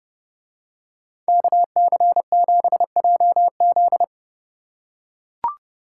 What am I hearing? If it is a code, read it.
Morse code, heard as a keyed tone, about 29 wpm, 700 Hz. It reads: KC7JZ